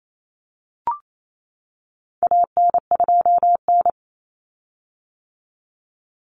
Morse code, heard as a keyed tone, about 28 wpm, 700 Hz. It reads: AN2N
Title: Common Cut Numbers - 4-Digit 28wpm